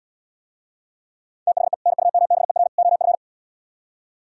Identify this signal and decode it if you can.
Morse code, heard as a keyed tone, about 76 wpm, 700 Hz. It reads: THE NUMBER OF